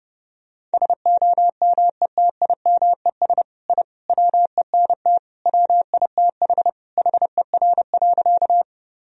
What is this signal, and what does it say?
Morse code, heard as a keyed tone, about 30 wpm, 700 Hz. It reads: SOMETIMES I WENT WITH HER.
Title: Sentences from Top 1000 Words - Review Part 3 - 30wpm